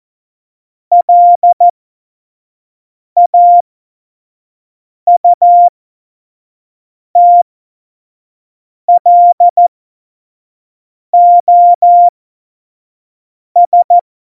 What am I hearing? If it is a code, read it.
Morse code, heard as a keyed tone, about 14 wpm, spaced as for 5 wpm, 700 Hz. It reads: LAUTLOS
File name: Morsekurs / Witze / Witz 28 April 2020, 25 F70